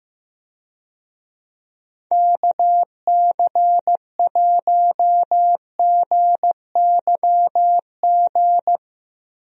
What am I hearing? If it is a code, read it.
Morse code, heard as a keyed tone, about 15 wpm, 700 Hz. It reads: KC1GYG
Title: USA Call Signs - 6 Characters - Repeated 2x - 15wpm